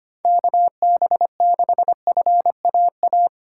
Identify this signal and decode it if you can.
Morse code, heard as a keyed tone, about 25 wpm, 700 Hz. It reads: KB6FAA
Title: USA Call Signs - 6 Characters - 3x Character Spacing - 25wpm